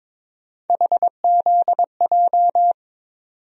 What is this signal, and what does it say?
Morse code, heard as a keyed tone, about 22 wpm, 700 Hz. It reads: HZJ
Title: USA Call Sign Postfixes - 3 Letters 22wpm